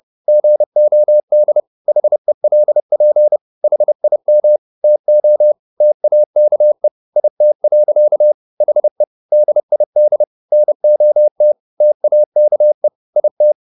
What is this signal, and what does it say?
Morse code, heard as a keyed tone, about 30 wpm, 600 Hz. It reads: GOD HELP HIM TO TAKE IT. HE DID NOT TAKE IT
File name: the old man and the sea / the old man and the sea 1 30wpm